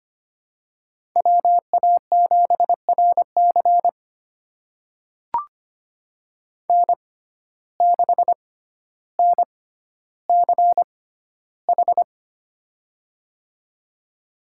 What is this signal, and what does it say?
Morse code, heard as a keyed tone, about 25 wpm, spaced as for 13 wpm, 700 Hz. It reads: WA7RC N6NCH
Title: USA Call Signs - 5 Characters - 6x Character Spacing - 25wpm